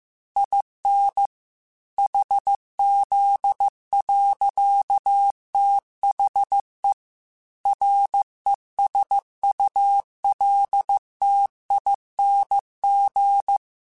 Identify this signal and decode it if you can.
Morse code, heard as a keyed tone, about 15 wpm, 800 Hz. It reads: IN HZ. THE RESULTING